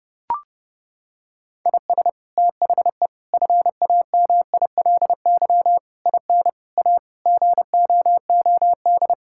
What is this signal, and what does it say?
Morse code, heard as a keyed tone, about 30 wpm, 700 Hz. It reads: IS THE FAMILY IN A GOOD
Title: Sentences from Top 1000 Words - Review Part 5 - 30wpm